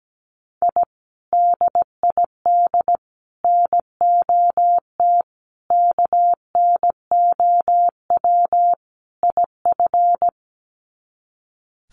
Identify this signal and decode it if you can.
Morse code, heard as a keyed tone, about 17 wpm, 700 Hz. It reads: I DID NOT KNOW IF
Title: Sets of 5 Words Encore 17wpm